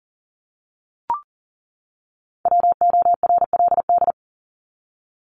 Morse code, heard as a keyed tone, about 40 wpm, 700 Hz. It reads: WORLD
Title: Top 200 Words 40wpm